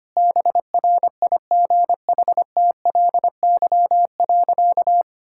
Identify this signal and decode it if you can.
Morse code, heard as a keyed tone, about 25 wpm, 700 Hz. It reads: BRIGHTLY.
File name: Easy Sentences with Hints - 25wpm